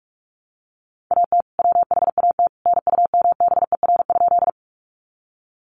Morse code, heard as a keyed tone, about 44 wpm, 700 Hz. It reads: AT WHAT NUMBER?
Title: Sentences from Top 100 Words 45wpm